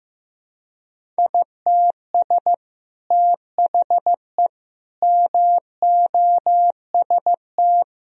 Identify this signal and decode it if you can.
Morse code, heard as a keyed tone, about 15 wpm, 700 Hz. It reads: ITS THE MOST
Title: Sentences from Top 1000 Words - Review Part 4 - 15wpm